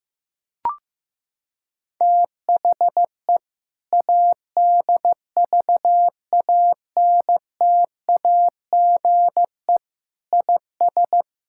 Morse code, heard as a keyed tone, about 15 wpm, 700 Hz. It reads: THE ADVANTAGE IS
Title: Sentences from Top 1000 Words - Review Part 6 - 15wpm